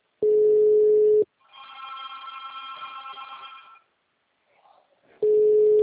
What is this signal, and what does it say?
Ringback tone (Europe, 425 Hz).